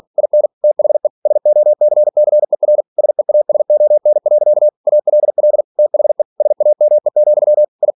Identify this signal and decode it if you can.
Morse code, heard as a keyed tone, about 46 wpm, 600 Hz. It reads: IN THE SOCCER SEASON. ALL THE SAME, I
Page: Never 47wpm